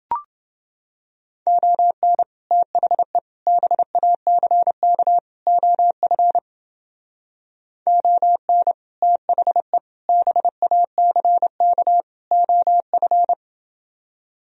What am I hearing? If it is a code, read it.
Morse code, heard as a keyed tone, about 30 wpm, 700 Hz. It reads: ON THE BACK OF ON THE BACK OF